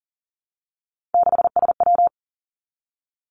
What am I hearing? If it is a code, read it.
Morse code, heard as a keyed tone, about 42 wpm, 700 Hz. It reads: 6SW
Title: Three Challenging Letter-Number Combinations 40wpm